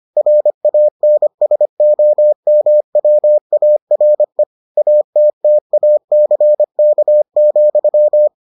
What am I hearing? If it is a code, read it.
Morse code, heard as a keyed tone, about 25 wpm, 600 Hz. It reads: RANSOMWARE ATTACK,